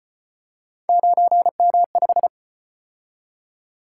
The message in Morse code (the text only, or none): 9M5